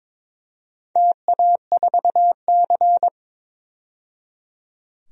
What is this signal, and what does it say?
Morse code, heard as a keyed tone, about 22 wpm, 700 Hz. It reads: TA4C
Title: DXCC-100 Call Signs - 4 Characters - 22wpm